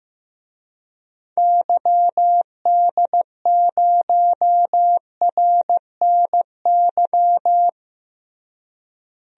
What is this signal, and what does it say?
Morse code, heard as a keyed tone, about 15 wpm, 700 Hz. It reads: YD0RNY
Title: DXCC-100 Call Signs - 6 Characters - Repeated 3x - 15wpm